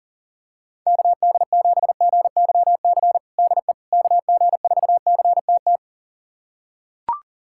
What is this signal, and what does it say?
Morse code, heard as a keyed tone, about 40 wpm, 700 Hz. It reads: KD7GYC DE KG4CTT
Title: USA Callsign DE Callsign - 6 Characters - 40wpm